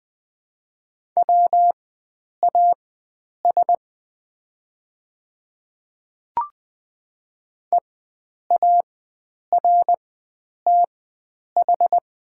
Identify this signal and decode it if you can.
Morse code, heard as a keyed tone, about 20 wpm, spaced as for 9 wpm, 700 Hz. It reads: WAS EARTH
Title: 4x Character Spacing - Top 300 Words - 20wpm